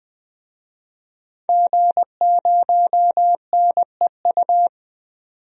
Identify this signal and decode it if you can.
Morse code, heard as a keyed tone, about 21 wpm, 700 Hz. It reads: G0NEU
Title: DXCC-100 Call Signs - 5 Characters - 20wpm